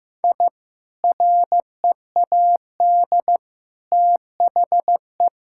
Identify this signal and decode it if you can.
Morse code, heard as a keyed tone, about 15 wpm, 700 Hz. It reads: I READ THE
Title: Sentences from Top 1000 Words - Review Part 6 - 15wpm